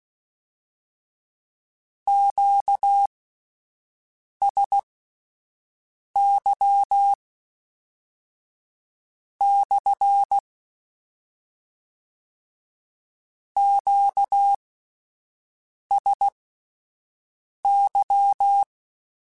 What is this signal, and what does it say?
Morse code, heard as a keyed tone, about 16 wpm, spaced as for 5 wpm, 800 Hz. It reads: QSY / QSY
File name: Q Calls / Q Calls-mp3s / Q CALLS SPSL @10